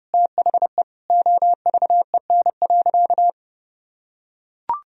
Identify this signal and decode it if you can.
Morse code, heard as a keyed tone, about 31 wpm, 700 Hz. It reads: THE OVEN.
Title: Sentences from Top 1000 Words - Review Part 4 - 30wpm